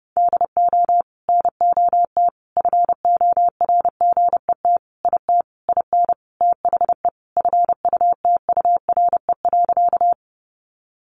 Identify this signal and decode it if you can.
Morse code, heard as a keyed tone, about 30 wpm, 700 Hz. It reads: DO NOT FORGET IT IN THE FUTURE.
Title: Sentences from Top 1200 Words 30wpm